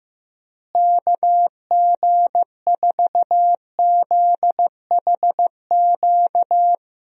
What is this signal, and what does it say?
Morse code, heard as a keyed tone, about 15 wpm, 700 Hz. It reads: KG4ZHQ
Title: USA Call Signs - 6 Characters - Repeated 3x - 15wpm